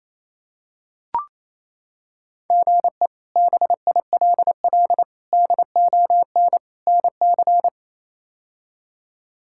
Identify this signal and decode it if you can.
Morse code, heard as a keyed tone, about 28 wpm, 700 Hz. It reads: GE BILL DON NC
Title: K1USN SST QSO Practice - 28wpm